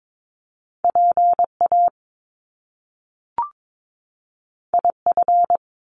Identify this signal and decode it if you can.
Morse code, heard as a keyed tone, about 22 wpm, 700 Hz. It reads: PA IF
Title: Top Two Letter Combinations in Words 22wpm